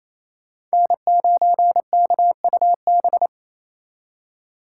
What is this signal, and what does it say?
Morse code, heard as a keyed tone, about 36 wpm, 700 Hz. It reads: N9KUB